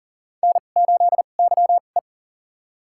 Morse code, heard as a keyed tone, about 40 wpm, spaced as for 29 wpm, 700 Hz. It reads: N8YE
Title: USA Call Signs - 2x Character Spacing - 40wpm